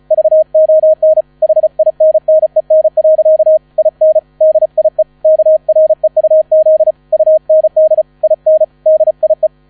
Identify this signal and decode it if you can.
Morse code, heard as a keyed tone, about 36 wpm, 630 Hz. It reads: VON HINNEN. IN DIE KREUZ UND IN DIE
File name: 35wpm